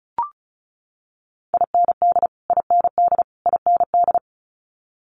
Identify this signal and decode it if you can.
Morse code, heard as a keyed tone, about 36 wpm, 700 Hz. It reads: IND IND IND
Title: Multi-letter Phonemes Repeated 3x 35wpm